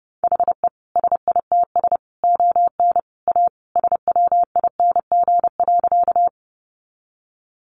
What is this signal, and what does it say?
Morse code, heard as a keyed tone, about 30 wpm, 700 Hz. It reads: HE SITS ON A SWING.
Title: Additional Easy Phrases 30wpm